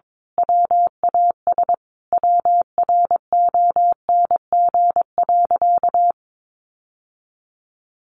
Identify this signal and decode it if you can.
Morse code, heard as a keyed tone, about 22 wpm, 700 Hz. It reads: WAS WRONG.
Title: Sentences from Top 1200 Words 22wpm